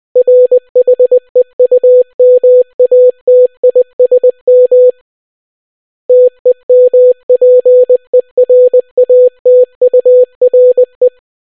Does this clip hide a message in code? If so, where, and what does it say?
Yes: Morse code from 0.2 to 11.1 s, heard as a keyed tone, about 20 wpm, 500 Hz. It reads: RHEUMATISM TEMPERATURE